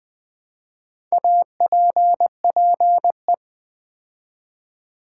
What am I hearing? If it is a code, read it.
Morse code, heard as a keyed tone, about 20 wpm, 700 Hz. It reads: APPE
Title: Build A Word - 2 Letters At A Time - 8-Letter Words - 20wpm